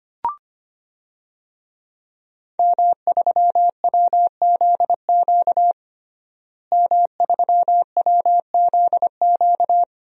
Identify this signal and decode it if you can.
Morse code, heard as a keyed tone, about 25 wpm, 700 Hz. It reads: M3WZQ M3WZQ
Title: DXCC-100 Call Signs - 5 Characters - Repeated 2x - 25wpm